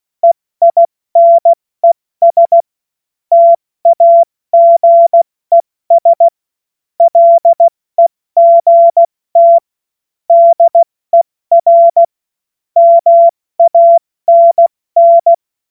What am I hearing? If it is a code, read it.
Morse code, heard as a keyed tone, about 16 wpm, 680 Hz. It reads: EINES TAGES LEGT DER MANN
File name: Morsekurs / Witze / Witz 15. November 2022 70 F80